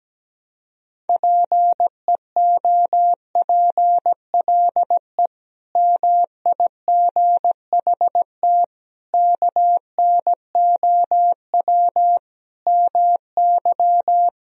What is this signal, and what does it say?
Morse code, heard as a keyed tone, about 17 wpm, 700 Hz. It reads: PEOPLE MIGHT KNOW MY
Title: Sentences from Top 500 Words and Abbreviations 17wpm